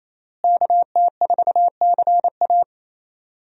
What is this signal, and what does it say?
Morse code, heard as a keyed tone, about 28 wpm, 700 Hz. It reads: KT4CA
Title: USA Call Signs - 5 Characters - Repeated 2x - 28wpm